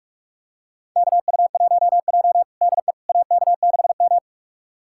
Morse code, heard as a keyed tone, about 41 wpm, 700 Hz. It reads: KU1J DE AK6M
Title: USA Callsign DE Callsign - 4 Characters - 45wpm